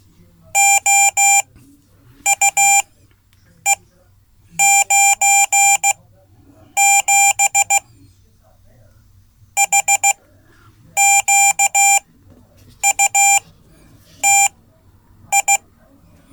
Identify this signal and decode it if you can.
Morse code, heard as a keyed tone, about 15 wpm, spaced as for 8 wpm, 790 Hz. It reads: OUE97 HQUTI
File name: Code / 5min-c